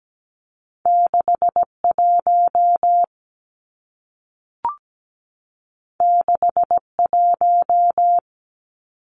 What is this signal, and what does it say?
Morse code, heard as a keyed tone, about 17 wpm, 700 Hz. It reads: 61 61
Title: Numbers - 2-Digit 17wpm